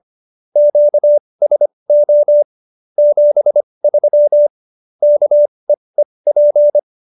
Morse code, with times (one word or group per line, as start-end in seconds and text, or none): 0.6-2.4 QSO
3.0-4.5 73
5.0-6.8 KEEP